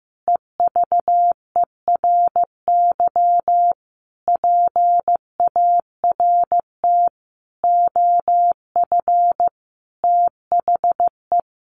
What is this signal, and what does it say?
Morse code, heard as a keyed tone, about 15 wpm, 700 Hz. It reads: EVERY PART OF THE
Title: Sentences from Top 1500 Words 15wpm